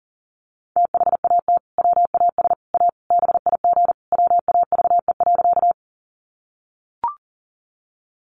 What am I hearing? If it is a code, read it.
Morse code, heard as a keyed tone, about 40 wpm, 700 Hz. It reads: THAT WAS A BIG WAVE.
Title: Sentences from Top 1400 Words 40wpm